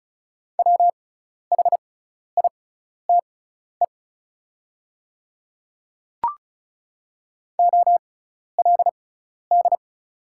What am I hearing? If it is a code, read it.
Morse code, heard as a keyed tone, about 36 wpm, spaced as for 12 wpm, 690 Hz. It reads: WHITE OLD
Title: 6x Character Spacing - Top 300 Words - 35wpm